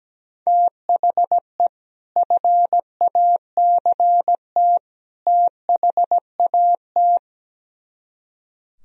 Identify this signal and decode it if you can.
Morse code, heard as a keyed tone, about 17 wpm, 700 Hz. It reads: THE FACT THAT
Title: Speed-Racing - Sets of 3 Words 17wpm